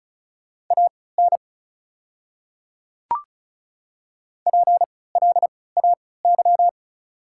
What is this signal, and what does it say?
Morse code, heard as a keyed tone, about 33 wpm, spaced as for 20 wpm, 690 Hz. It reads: AN PLAY
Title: 3x Character Spacing - Top 500 Words - 35wpm